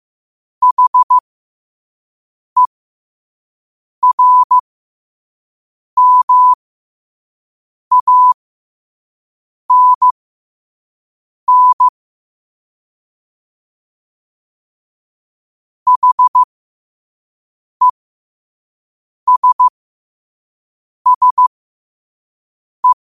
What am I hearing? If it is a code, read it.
Morse code, heard as a keyed tone, about 15 wpm, spaced as for 5 wpm, 1000 Hz. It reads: HERMANN HESSE
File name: qotdfiles / 5wpm